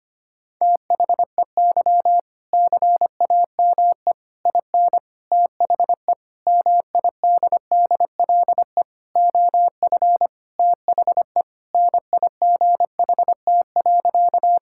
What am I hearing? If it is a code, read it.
Morse code, heard as a keyed tone, about 25 wpm, 700 Hz. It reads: THEY CAME IN THE MIDDLE OF THE NIGHT.
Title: Sentences from Top 1000 Words - Review Part 6 - 25wpm